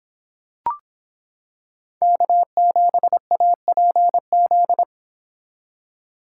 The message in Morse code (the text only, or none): K7APZ